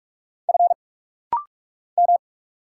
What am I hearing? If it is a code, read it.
Morse code, heard as a keyed tone, about 46 wpm, 700 Hz. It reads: F M
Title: Single Letters - Rapid-Fire 45wpm